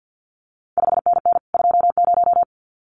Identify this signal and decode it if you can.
Morse code, heard as a keyed tone, about 50 wpm, 700 Hz. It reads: 5NN 20
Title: CQ WW DX Contest Exchange - 50wpm